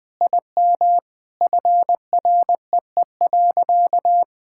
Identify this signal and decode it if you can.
Morse code, heard as a keyed tone, about 21 wpm, 700 Hz. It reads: IM FREE.
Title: Sentences with 3-Word Phrases - 20wpm